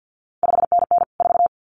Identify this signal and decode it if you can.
Morse code, heard as a keyed tone, about 50 wpm, 700 Hz. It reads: 5NN 4
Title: CQ WW DX Contest Exchange - 50wpm